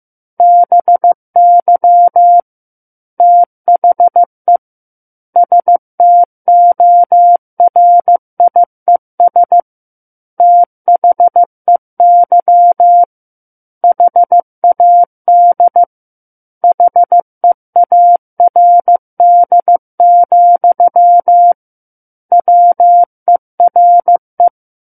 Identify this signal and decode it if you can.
Morse code, heard as a keyed tone, about 15 wpm, 700 Hz. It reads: BY THE STORIES THEY HAD HEARD, WERE
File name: Book One / 06-Chapter 6 - 15 WPM